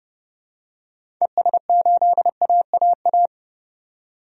Morse code, heard as a keyed tone, about 30 wpm, 700 Hz. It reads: ES8AAA
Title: DXCC-100 Call Signs - 6 Characters - 30wpm